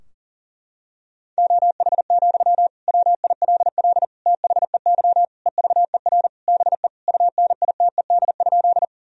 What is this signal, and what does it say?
Morse code, heard as a keyed tone, about 40 wpm, 700 Hz. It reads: OH, WILL THEY EVER BE UNITED?
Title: Sentences from Top 1000 Words - Review Part 4 - 40wpm